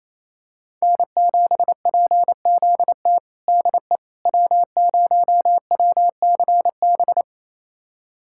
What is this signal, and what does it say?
Morse code, heard as a keyed tone, about 28 wpm, 700 Hz. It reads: N7PZT DE W0WCB
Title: USA Callsign DE Callsign 28wpm